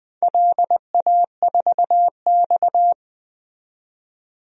Morse code, heard as a keyed tone, about 20 wpm, 700 Hz. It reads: LA4X
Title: DXCC-100 Call Signs - 4 Characters - 20wpm